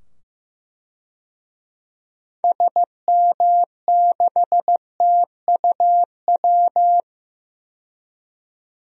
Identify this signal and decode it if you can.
Morse code, heard as a keyed tone, about 15 wpm, 700 Hz. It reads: SM6TUW
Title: DXCC-100 Call Signs - 6 Characters - 15wpm